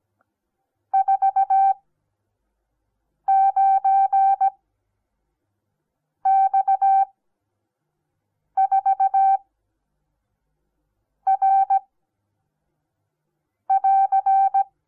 Morse code, heard as a keyed tone, about 17 wpm, spaced as for 5 wpm, 780 Hz. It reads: 49X4R <AR>
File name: Onlinelehrgang / morsen / DJ4UF Morsen08